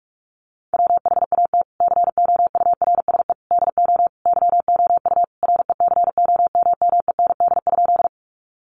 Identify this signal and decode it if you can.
Morse code, heard as a keyed tone, about 44 wpm, 700 Hz. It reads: WHAT COURSE DO YOU RECOMMEND?
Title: Sentences from Top 1200 Words 45wpm